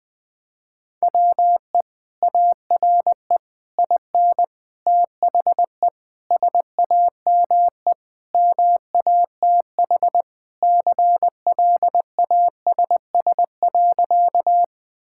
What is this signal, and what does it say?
Morse code, heard as a keyed tone, about 20 wpm, 700 Hz. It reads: WE ARE IN THE SAME MATH CLASS.
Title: Sentences from Top 1000 Words - Review Part 6 - 20wpm